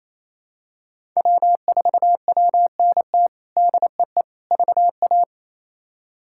Morse code, heard as a keyed tone, about 28 wpm, 700 Hz. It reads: W4WNT DEE VA
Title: K1USN SST QSO Practice - 28wpm